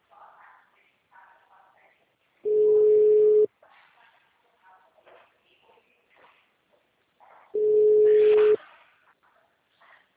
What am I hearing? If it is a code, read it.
Ringback tone (Europe, 425 Hz).
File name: out- -2041- - - .1243